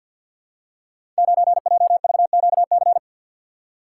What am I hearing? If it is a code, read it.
Morse code, heard as a keyed tone, about 52 wpm, 690 Hz. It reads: 9JVQC
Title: Code Groups - 5 Letter:Numbers - 50wpm